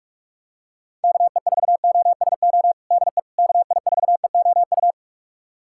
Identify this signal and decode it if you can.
Morse code, heard as a keyed tone, about 46 wpm, 690 Hz. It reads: KE4OSO DE KI4EOU
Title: USA Callsign DE Callsign - 6 Characters - 45wpm